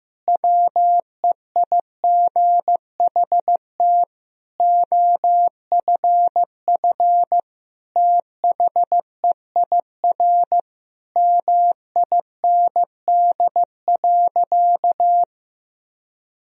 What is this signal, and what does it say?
Morse code, heard as a keyed tone, about 15 wpm, 700 Hz. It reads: WEIGHT OFF THEIR MIND.
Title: Sentences from Top 1000 Words - Review Part 3 - 15wpm